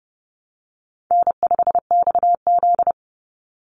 Morse code, heard as a keyed tone, about 30 wpm, 700 Hz. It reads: N5XZ
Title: Easy Call Signs Repeated 2x - 30wpm